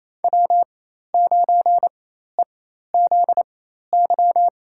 Morse code, heard as a keyed tone, about 28 wpm, spaced as for 13 wpm, 700 Hz. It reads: W9EZY
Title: USA Call Signs - 5 Characters - 4x Character Spacing - 28wpm